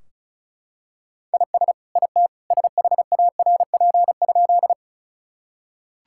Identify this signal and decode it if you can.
Morse code, heard as a keyed tone, about 34 wpm, 700 Hz. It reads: IS IT SHARP?
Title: Sentences from Top 1600 Words 35wpm